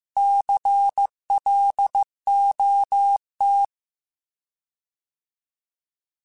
Morse code, heard as a keyed tone, about 15 wpm, 800 Hz. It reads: CLOT
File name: Alphabetical / Alphabetical mp3s / Cs @13